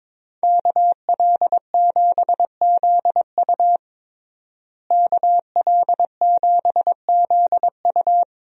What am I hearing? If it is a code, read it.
Morse code, heard as a keyed tone, about 22 wpm, 700 Hz. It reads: KL7ZU KL7ZU
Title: USA Call Signs - 5 Characters - Repeated 2x - 22wpm